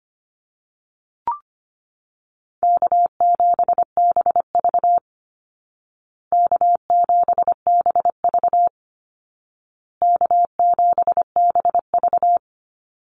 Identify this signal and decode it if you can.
Morse code, heard as a keyed tone, about 25 wpm, 700 Hz. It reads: K7BV K7BV K7BV
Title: Easy Call Signs Repeated 3x - 25wpm